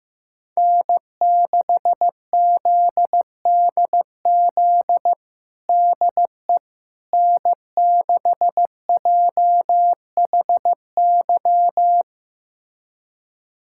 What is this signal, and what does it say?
Morse code, heard as a keyed tone, about 15 wpm, 700 Hz. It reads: N6ZDZ DE N6JHY
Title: USA Callsign DE Callsign - 5 Characters - 15wpm